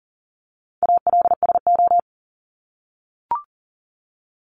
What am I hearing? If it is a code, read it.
Morse code, heard as a keyed tone, about 40 wpm, 700 Hz. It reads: ALSO
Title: Top 500 Words 40wpm